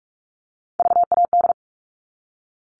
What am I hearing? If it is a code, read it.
Morse code, heard as a keyed tone, about 44 wpm, 700 Hz. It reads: VAD
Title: All Cut Numbers - 3-Digit 45wpm